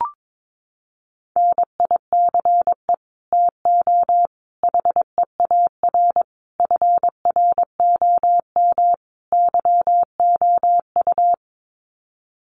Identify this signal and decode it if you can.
Morse code, heard as a keyed tone, about 22 wpm, 700 Hz. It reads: NICE TO HEAR FROM YOU
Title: Common QSO Phrases 22wpm